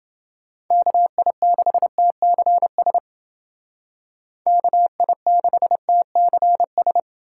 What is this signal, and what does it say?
Morse code, heard as a keyed tone, about 29 wpm, 700 Hz. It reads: KI6TCS KI6TCS